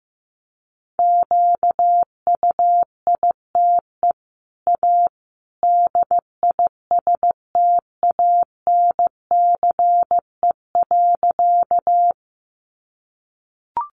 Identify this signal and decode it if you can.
Morse code, heard as a keyed tone, about 15 wpm, 700 Hz. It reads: QUITE A DISTANCE.
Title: Sentences from Top 1200 Words 15wpm